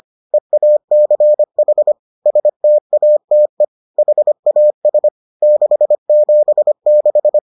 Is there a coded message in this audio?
Yes: Morse code from 0.3 to 7.4 s, heard as a keyed tone, about 25 wpm, 600 Hz. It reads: EACH STATE HAS 676